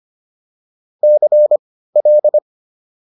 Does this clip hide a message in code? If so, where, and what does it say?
Yes: Morse code from 1.0 to 2.4 s, heard as a keyed tone, about 25 wpm, spaced as for 15 wpm, 600 Hz. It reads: CL